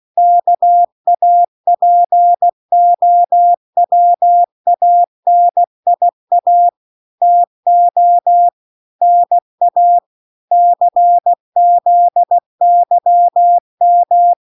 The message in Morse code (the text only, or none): KAPOWANIA TO NA CZYM